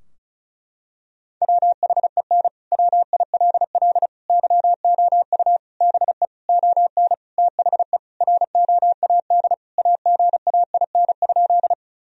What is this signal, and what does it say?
Morse code, heard as a keyed tone, about 34 wpm, 700 Hz. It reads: WHEN WILL YOU BE ON THE ROAD AGAIN?
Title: Sentences from Top 1000 Words - Review Part 2 - 35wpm